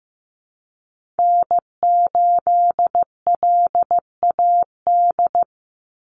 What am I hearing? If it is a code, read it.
Morse code, heard as a keyed tone, about 15 wpm, 700 Hz. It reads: N8LAD
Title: USA Call Signs 15wpm